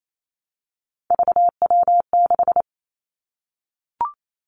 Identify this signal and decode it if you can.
Morse code, heard as a keyed tone, about 28 wpm, 700 Hz. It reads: VW6
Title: Three Challenging Letter-Number Combinations 28wpm